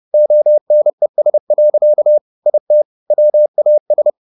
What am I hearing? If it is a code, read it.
Morse code, heard as a keyed tone, about 30 wpm, 600 Hz. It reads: ONES. IT WAS